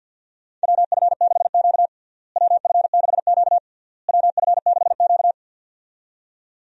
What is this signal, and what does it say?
Morse code, heard as a keyed tone, about 47 wpm, 700 Hz. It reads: WF6Q WF6Q WF6Q
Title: USA Call Signs - 4 Characters - Repeated 3x - 50wpm